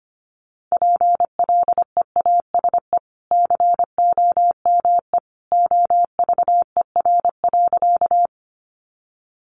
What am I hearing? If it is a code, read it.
Morse code, heard as a keyed tone, about 25 wpm, 700 Hz. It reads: PLEASE COME OVER.
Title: Sentences from Top 1500 Words 25wpm